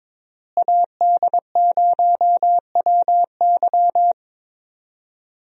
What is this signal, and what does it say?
Morse code, heard as a keyed tone, about 22 wpm, 700 Hz. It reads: AD0WY
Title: USA Call Signs - 5 Characters - Repeated 3x - 22wpm